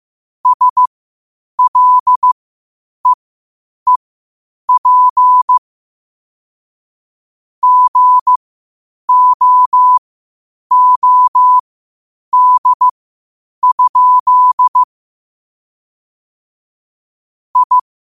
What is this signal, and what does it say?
Morse code, heard as a keyed tone, about 15 wpm, spaced as for 8 wpm, 1000 Hz. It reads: SLEEP GOOD? I
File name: qotdfiles / 8wpm